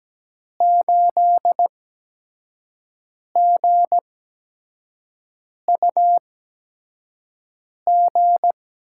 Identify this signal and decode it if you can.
Morse code, heard as a keyed tone, about 17 wpm, spaced as for 5 wpm, 700 Hz. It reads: 8GUG
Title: USA Call Signs - 8x Character Spacing - 17wpm